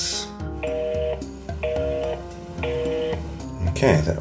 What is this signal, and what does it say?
Telephone call progress tone (North America), heard as a telephone network's tone, about 480, 620 Hz, with speech. Busy tone.